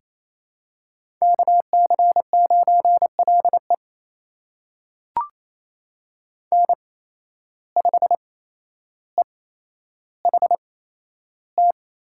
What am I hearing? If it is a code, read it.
Morse code, heard as a keyed tone, about 28 wpm, spaced as for 12 wpm, 700 Hz. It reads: KC9LE N5EHT
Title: USA Call Signs - 5 Characters - 8x Character Spacing - 28wpm